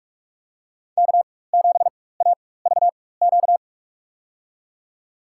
Morse code, heard as a keyed tone, about 46 wpm, spaced as for 21 wpm, 700 Hz. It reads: K7AVQ
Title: USA Call Signs - 5 Characters - 4x Character Spacing - 45wpm